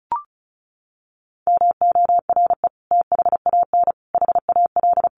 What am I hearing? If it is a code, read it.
Morse code, heard as a keyed tone, about 34 wpm, 700 Hz. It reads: MORE THAN HAL
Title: Sets of 5 Words 35wpm